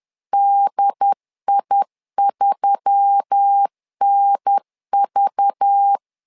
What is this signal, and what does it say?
Morse code, heard as a keyed tone, about 10 wpm, 800 Hz. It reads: DI3NV
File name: LEZIONI / 5 LBVC / 5 LBV 10wpm 2248